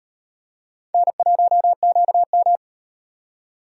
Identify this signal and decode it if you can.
Morse code, heard as a keyed tone, about 38 wpm, 700 Hz. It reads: N1QM